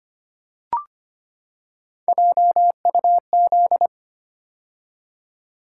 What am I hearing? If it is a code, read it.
Morse code, heard as a keyed tone, about 25 wpm, 700 Hz. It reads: JUZ
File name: Three Letter-Number Combinations - 25wpm